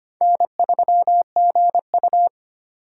Morse code, heard as a keyed tone, about 25 wpm, 700 Hz. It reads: N3GU
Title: USA Call Signs - 4 Characters - 2x Character Spacing - 25wpm